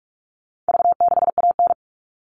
Morse code, heard as a keyed tone, about 44 wpm, 700 Hz. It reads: V6AN
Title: All Cut Numbers - 4-Digit 45wpm